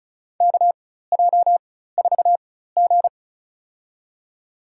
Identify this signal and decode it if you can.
Morse code, heard as a keyed tone, about 34 wpm, spaced as for 16 wpm, 700 Hz. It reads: KJ4G
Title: USA Call Signs - 4 Characters - 4x Character Spacing - 35wpm